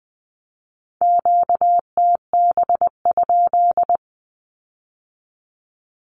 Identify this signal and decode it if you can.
Morse code, heard as a keyed tone, about 20 wpm, 700 Hz. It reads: QTB?
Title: Q-Codes with Spoken Meaning - 20wpm